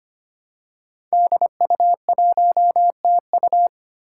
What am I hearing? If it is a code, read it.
Morse code, heard as a keyed tone, about 25 wpm, 700 Hz. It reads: DU1TU
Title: Five Challenging Letter-Number Combinations 25wpm